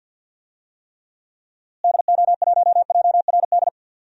Morse code, heard as a keyed tone, about 52 wpm, 700 Hz. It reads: DO1JRD
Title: DXCC-100 Call Signs - 6 Characters - Repeated 3x - 50wpm